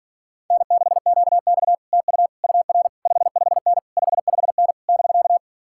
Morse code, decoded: N6QX TU UR 55N 55N <BK>

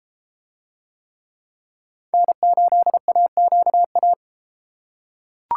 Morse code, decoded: N8AQA

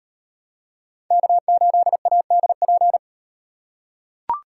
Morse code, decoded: K8ADP

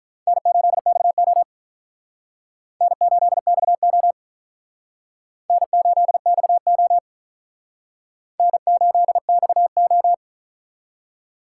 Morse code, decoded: N8XO N8XO N8XO N8XO

